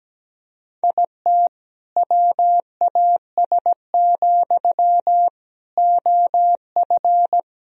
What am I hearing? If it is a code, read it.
Morse code, heard as a keyed tone, about 17 wpm, 700 Hz. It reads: IT WAS, OF
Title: Sentences from Top 1000 Words - Review Part 3 - 17wpm